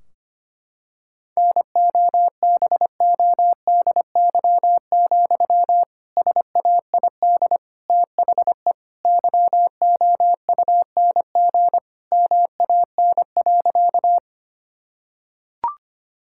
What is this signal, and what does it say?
Morse code, heard as a keyed tone, about 25 wpm, 700 Hz. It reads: NOBODY, SAID THE YOUNG MAN.
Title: Sentences from Top 1600 Words 25wpm